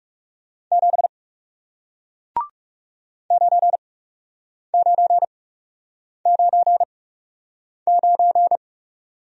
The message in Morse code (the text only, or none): 7 9999